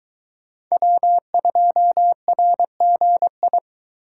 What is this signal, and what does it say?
Morse code, heard as a keyed tone, about 22 wpm, 700 Hz. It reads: W2RGI